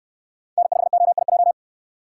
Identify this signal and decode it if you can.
Morse code, heard as a keyed tone, about 69 wpm, 700 Hz. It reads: N5OIJ